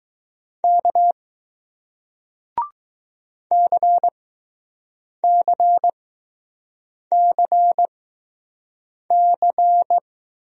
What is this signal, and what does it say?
Morse code, heard as a keyed tone, about 20 wpm, spaced as for 6 wpm, 700 Hz. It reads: K CCCC